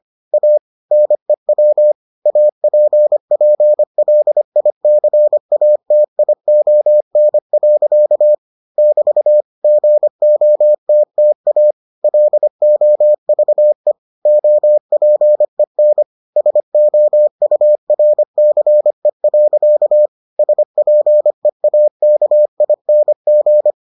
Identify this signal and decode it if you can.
Morse code, heard as a keyed tone, about 25 wpm, 600 Hz. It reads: A NEW APPLICATION. <BT> GOTTA LOVE OPEN SOURCE. SPEAKING